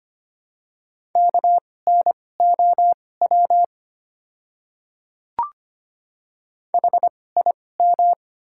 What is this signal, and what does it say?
Morse code, heard as a keyed tone, about 25 wpm, spaced as for 18 wpm, 700 Hz. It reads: KNOW HIM
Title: 2x Character Spacing - Top 300 Words - 25wpm